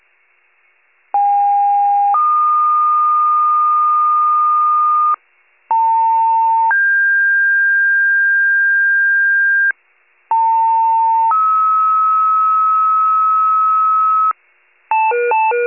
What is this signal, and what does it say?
Two-tone page, heard as a one-second tone, then a three-second tone, about 810 then 1220 Hz, about 900 then 1600 Hz, about 900 then 1290 Hz. It is heard from 1.1 to 14.3 s.
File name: wcalert